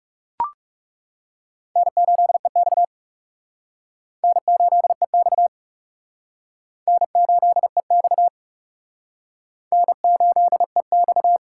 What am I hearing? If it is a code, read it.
Morse code, heard as a keyed tone, about 36 wpm, 700 Hz. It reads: N8EX N8EX N8EX N8EX
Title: Speed Racing - USA 1x2 Call Signs 30wpm